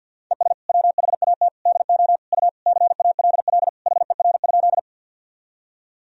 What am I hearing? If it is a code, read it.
Morse code, heard as a keyed tone, about 50 wpm, 700 Hz. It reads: ES WHAT DO U CALL HER?
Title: Sentences from Top 100 Words and Abbreviations 50wpm